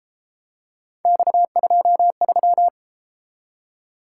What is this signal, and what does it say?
Morse code, heard as a keyed tone, about 33 wpm, 700 Hz. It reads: X23